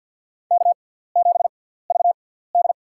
Morse code, heard as a keyed tone, about 52 wpm, spaced as for 17 wpm, 690 Hz. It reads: K7VD
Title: USA Call Signs - 4 Characters - 6x Character Spacing - 50wpm